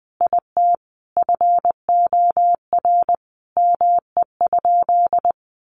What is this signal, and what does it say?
Morse code, heard as a keyed tone, about 20 wpm, 700 Hz. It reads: IT FOR ME?
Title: Sentences from Top 1400 Words 20wpm